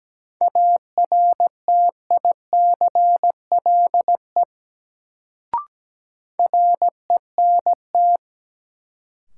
Morse code, heard as a keyed tone, about 17 wpm, 700 Hz. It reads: ARTICLE RENT
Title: Sentences from Top 1000 Words - Review Part 5 - New Word Drill - 17wpm